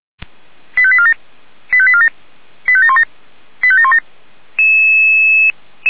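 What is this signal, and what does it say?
Five-tone selective call (ZVEI), heard as quick runs of five tones, tones of about 70 ms. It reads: 85847 85847 85717 85717
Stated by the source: Five-tone ZVEI selective call, code 85752, received by SDR.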